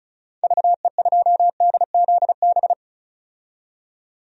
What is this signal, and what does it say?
Morse code, heard as a keyed tone, about 36 wpm, 700 Hz. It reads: VE2DZB
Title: DXCC-100 Call Signs - 6 Characters - 35wpm